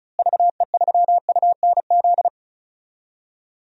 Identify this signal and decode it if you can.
Morse code, heard as a keyed tone, about 36 wpm, 700 Hz. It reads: VE3UNZ
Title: DXCC-100 Call Signs - 6 Characters - Repeated 2x - 35wpm